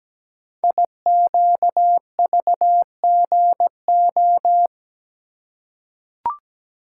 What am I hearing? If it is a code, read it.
Morse code, heard as a keyed tone, about 17 wpm, 700 Hz. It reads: IQVGO
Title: Code Groups - 5 Letters - 17wpm